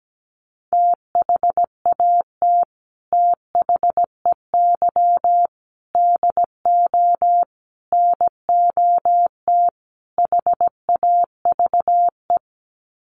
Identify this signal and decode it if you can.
Morse code, heard as a keyed tone, about 17 wpm, 700 Hz. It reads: THAT THEY DO NOT HAVE
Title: Sets of 5 Words Encore2 17wpm